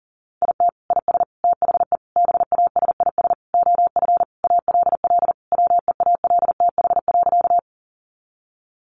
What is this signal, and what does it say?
Morse code, heard as a keyed tone, about 37 wpm, 700 Hz. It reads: IT IS THE BASIS OF ALL WEALTH.
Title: Sentences from Top 1200 Words 40wpm